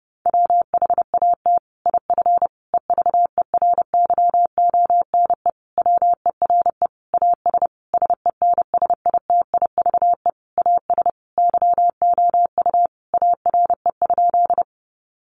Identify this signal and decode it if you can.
Morse code, heard as a keyed tone, about 30 wpm, 700 Hz. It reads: WHAT IF EVERYONE WERE AS SENSITIVE AS YOU ARE?
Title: Sentences from Top 1400 Words 30wpm